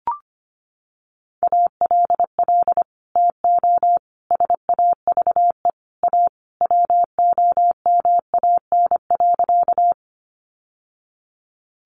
Morse code, heard as a keyed tone, about 25 wpm, 700 Hz. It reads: ALL TO SAVE A WOMAN.
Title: Sentences from Top 1100 Words 25wpm